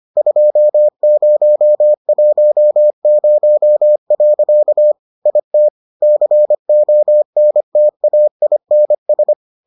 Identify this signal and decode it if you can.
Morse code, heard as a keyed tone, about 25 wpm, 600 Hz. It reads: 2010. IT CONTAINS